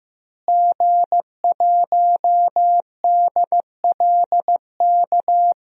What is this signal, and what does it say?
Morse code, heard as a keyed tone, about 15 wpm, 700 Hz. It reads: G1DLK
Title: DXCC-100 Call Signs - 5 Characters - Repeated 2x - 15wpm